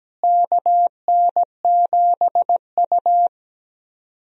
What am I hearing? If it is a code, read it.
Morse code, heard as a keyed tone, about 17 wpm, 700 Hz. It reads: KN7U
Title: USA Call Signs - 4 Characters - Repeated 3x - 17wpm